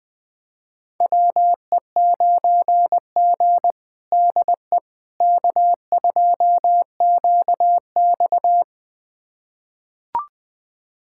Morse code, heard as a keyed tone, about 20 wpm, 700 Hz. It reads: WE9G DE K2QX
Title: USA Callsign DE Callsign - 4 Characters - 20wpm